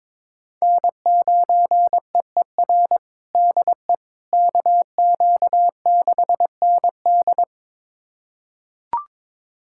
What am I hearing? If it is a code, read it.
Morse code, heard as a keyed tone, about 22 wpm, 700 Hz. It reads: N9EER DE KQ6ND
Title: USA Callsign DE Callsign - 5 Characters - 22wpm